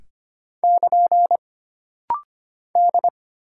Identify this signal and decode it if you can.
Morse code, heard as a keyed tone, about 25 wpm, 700 Hz. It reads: <KN> D
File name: Single Character (Letter:Number:Punctuation:Prosign) - Rapid-Fire - 25wpm